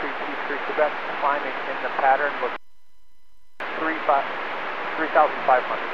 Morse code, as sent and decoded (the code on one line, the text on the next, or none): none